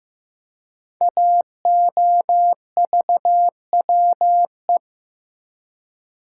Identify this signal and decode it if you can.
Morse code, heard as a keyed tone, about 15 wpm, 700 Hz. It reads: AOVWE
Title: Code Groups - 5 Letters - 15wpm